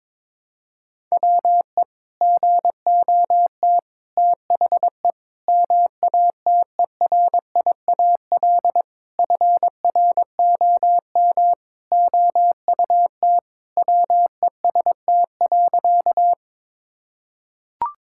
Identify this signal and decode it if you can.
Morse code, heard as a keyed tone, about 21 wpm, 700 Hz. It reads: WE GOT THE MATERIAL FROM OUT WEST.
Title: Sentences from Top 1000 Words - Review Part 5 - 22wpm